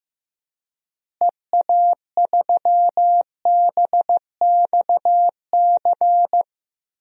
Morse code, heard as a keyed tone, about 15 wpm, 700 Hz. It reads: EA3BXC
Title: DXCC-100 Call Signs - 6 Characters - Repeated 3x - 15wpm